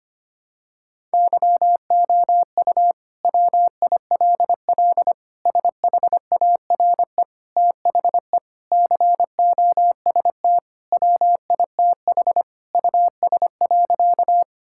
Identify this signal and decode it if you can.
Morse code, heard as a keyed tone, about 25 wpm, 700 Hz. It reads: YOU WILL SHARE THE COST WITH US.
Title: Sentences from Top 1000 Words - Review Part 6 - 25wpm